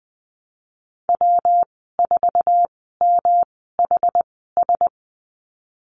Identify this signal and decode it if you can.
Morse code, heard as a keyed tone, about 20 wpm, spaced as for 15 wpm, 700 Hz. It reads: W4MHS
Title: 122 Lesson 20g - Number 3 - Call Signs 20wpm